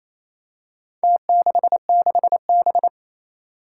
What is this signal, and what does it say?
Morse code, heard as a keyed tone, about 28 wpm, 700 Hz. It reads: T66B
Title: Four Challenging Letter-Number Combinations 28wpm